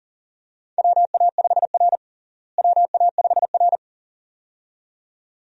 Morse code, heard as a keyed tone, about 37 wpm, 700 Hz. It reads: WA5R WA5R
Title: USA Call Signs - 4 Characters - Repeated 2x - 40wpm